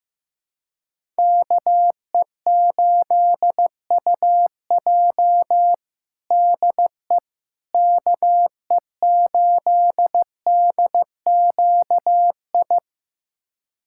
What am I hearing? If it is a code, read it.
Morse code, heard as a keyed tone, about 16 wpm, 700 Hz. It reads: KE8UJ DE KE8DQI
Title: USA Callsign DE Callsign 15wpm